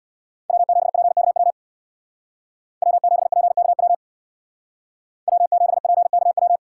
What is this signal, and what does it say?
Morse code, heard as a keyed tone, about 64 wpm, 700 Hz. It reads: W7PKW W7PKW W7PKW